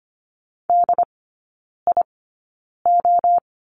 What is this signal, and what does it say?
Morse code, heard as a keyed tone, about 25 wpm, spaced as for 9 wpm, 690 Hz. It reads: DIO
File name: Lesson 14e - Number 2 - Three Characters 25wpm